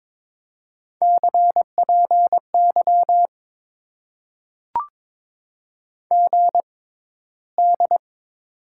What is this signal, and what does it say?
Morse code, heard as a keyed tone, about 22 wpm, spaced as for 14 wpm, 700 Hz. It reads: CPY GD
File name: Top 90 QSO Elements - 6x Character Spacing - 22wpm